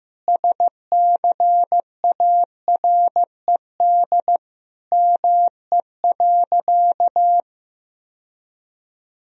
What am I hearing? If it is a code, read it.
Morse code, heard as a keyed tone, about 14 wpm, 700 Hz. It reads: SCARED ME.
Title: Sentences from Top 1000 Words - Review Part 4 - 15wpm